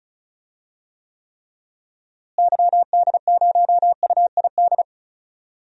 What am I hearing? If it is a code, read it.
Morse code, heard as a keyed tone, about 34 wpm, 700 Hz. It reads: YD0UID
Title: DXCC-100 Call Signs - 6 Characters - Repeated 3x - 35wpm